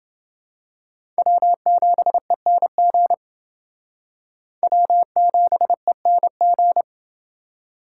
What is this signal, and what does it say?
Morse code, heard as a keyed tone, about 28 wpm, 700 Hz. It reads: W7ENG W7ENG